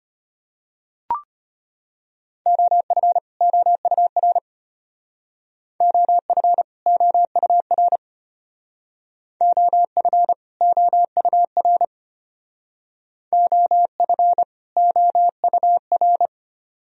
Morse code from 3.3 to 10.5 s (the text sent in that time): OUR OF OUR OF